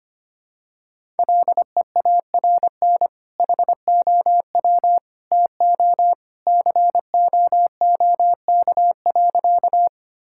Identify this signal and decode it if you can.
Morse code, heard as a keyed tone, about 25 wpm, 700 Hz. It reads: LEARN HOW TO COOK.
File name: practice / 25wpm / Sentences with 2-Word Phrases - 25wpm